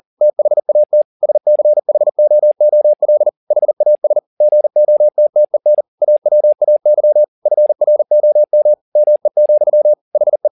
Morse code, heard as a keyed tone, about 40 wpm, 600 Hz. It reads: THAT SCHOOL HAS GOTTEN AWAY FROM ME, HE